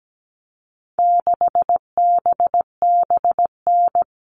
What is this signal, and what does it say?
Morse code, heard as a keyed tone, about 17 wpm, 700 Hz. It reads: 6BBN
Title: All Cut Numbers - 4-Digit 17wpm